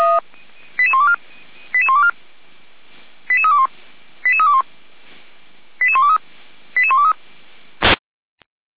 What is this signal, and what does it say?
Five-tone selective call (ZVEI), heard as quick runs of five tones, tones of about 70 ms. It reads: 88124 88124 88321 88321 88123 88123
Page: SDR/ZVEI